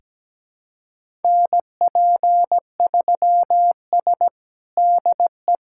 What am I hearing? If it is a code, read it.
Morse code, heard as a keyed tone, about 17 wpm, 700 Hz. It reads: NP3S DE
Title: USA Callsign DE Callsign - 4 Characters - 17wpm